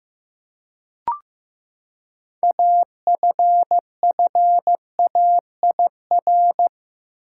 Morse code, heard as a keyed tone, about 15 wpm, 700 Hz. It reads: AFFAIR
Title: Six Letter Words - 15wpm